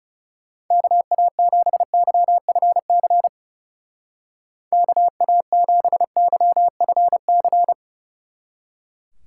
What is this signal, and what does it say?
Morse code, heard as a keyed tone, about 34 wpm, 700 Hz. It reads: KA7YFC KA7YFC
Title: Speed Racing - USA 2x3 Call Signs 30wpm